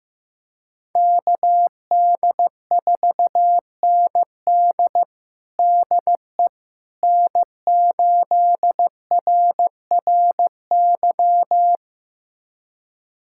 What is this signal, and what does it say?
Morse code, heard as a keyed tone, about 15 wpm, 700 Hz. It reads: KD4ND DE N8RRY
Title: USA Callsign DE Callsign - 5 Characters - 15wpm